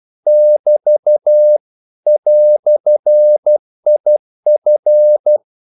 Morse code, heard as a keyed tone, about 12 wpm, 600 Hz. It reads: <BT> "IF
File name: CW / Practice / EnglishQuotations0312